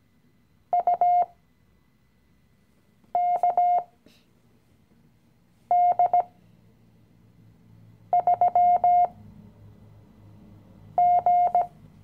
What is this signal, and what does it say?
Morse code, heard as a keyed tone, about 18 wpm, spaced as for 4 wpm, 710 Hz. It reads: UKD3G
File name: CW-Online-Kurs / Anfänger / Lektion 016